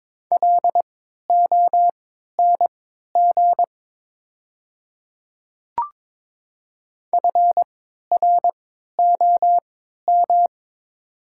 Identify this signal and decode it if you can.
Morse code, heard as a keyed tone, about 23 wpm, spaced as for 13 wpm, 700 Hz. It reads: LONG FROM
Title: 3x Character Spacing - Top 100 Words - 22wpm